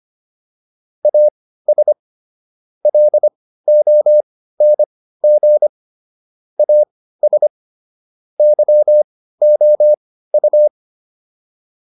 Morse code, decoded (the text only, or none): AS LONG AS YOU